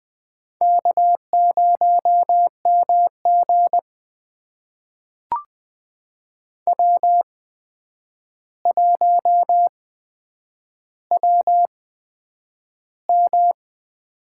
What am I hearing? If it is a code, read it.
Morse code, heard as a keyed tone, about 20 wpm, spaced as for 9 wpm, 700 Hz. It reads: K0MG W1WM
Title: USA Call Signs - 4 Characters - 8x Character Spacing - 20wpm